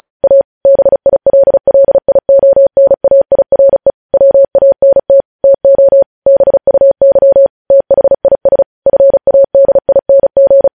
Morse code, heard as a keyed tone, about 36 wpm, 550 Hz. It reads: A BILLIONAIRE WANT TO BUY THIS FADING
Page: news headlines in morse code at 35 wpm